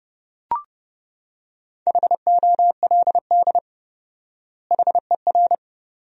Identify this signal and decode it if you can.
Morse code, heard as a keyed tone, about 30 wpm, 700 Hz. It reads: HOLD HER
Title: 4x Word-Spacing - Sentences from Top 400 Words - 30wpm